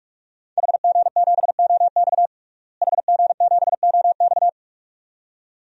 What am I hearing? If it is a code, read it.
Morse code, heard as a keyed tone, about 46 wpm, 700 Hz. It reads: HG7OX HG7OX
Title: DXCC-100 Call Signs - 5 Characters - Repeated 2x - 45wpm